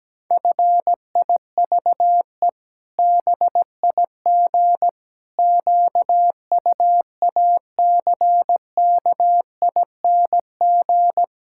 Morse code, heard as a keyed tone, about 17 wpm, 700 Hz. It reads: FIVE BIG QUACKING
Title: Pangram Challenge v2 - 17wpm